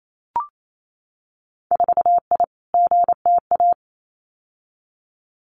Morse code, heard as a keyed tone, about 28 wpm, 700 Hz. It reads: 4I GTA
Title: Winter Field Day Exchanges Encore 28wpm